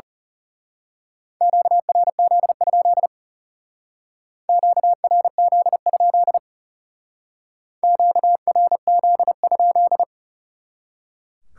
Morse code, heard as a keyed tone, about 36 wpm, 700 Hz. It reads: QRZ? QRZ? QRZ?
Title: Speed-Racing Top 180 QSO Elements 30wpm